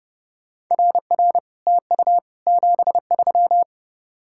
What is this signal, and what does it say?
Morse code, heard as a keyed tone, about 30 wpm, 700 Hz. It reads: RR TU 73